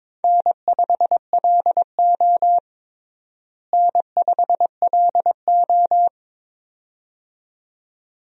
Morse code, heard as a keyed tone, about 22 wpm, 700 Hz. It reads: N5LO N5LO
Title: USA Call Signs - 4 Characters - Repeated 2x - 22wpm